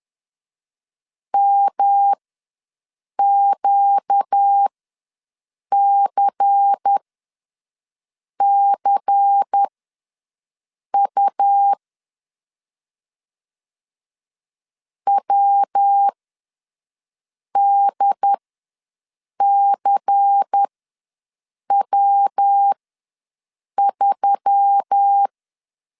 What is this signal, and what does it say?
Morse code, heard as a keyed tone, about 10 wpm, spaced as for 5 wpm, 800 Hz. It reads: MQCCU WDCW3
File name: LEZIONI / 5 LBVC / 5 LBVC ws 10wpm 3733